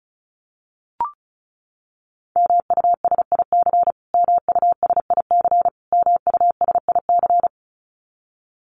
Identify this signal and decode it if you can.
Morse code, heard as a keyed tone, about 34 wpm, 700 Hz. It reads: MUSIC MUSIC MUSIC
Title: Top 300 Words Repeated 35wpm